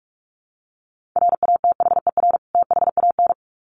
Morse code, heard as a keyed tone, about 46 wpm, 700 Hz. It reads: RATHER THAN
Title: Sets of 2 Words 45wpm